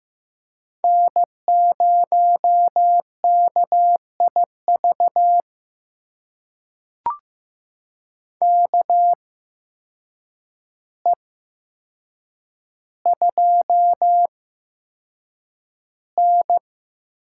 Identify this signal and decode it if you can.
Morse code, heard as a keyed tone, about 15 wpm, spaced as for 7 wpm, 700 Hz. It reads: N0KIV KE2N
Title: USA Call Signs - 8x Character Spacing - 15wpm